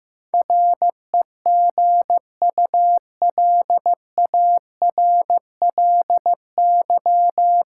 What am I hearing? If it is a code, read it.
Morse code, heard as a keyed tone, about 15 wpm, 700 Hz. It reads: REGULARLY
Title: Sentences from Top 1400 Words - New Word Drill - 15wpm